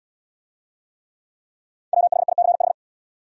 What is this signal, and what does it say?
Morse code, heard as a keyed tone, about 70 wpm, 700 Hz. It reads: WHICH